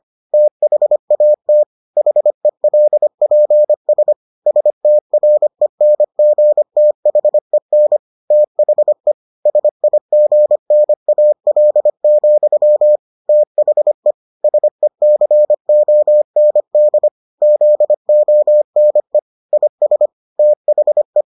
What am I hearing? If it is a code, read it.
Morse code, heard as a keyed tone, about 25 wpm, 600 Hz. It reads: THAT HELPS STRENGTHEN THE SIGNAL, THE SECOND ZONE IS THE